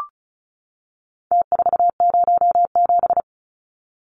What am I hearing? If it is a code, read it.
Morse code, heard as a keyed tone, about 34 wpm, 700 Hz. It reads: T407
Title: Numbers - Contest Style - 4-Digit 35wpm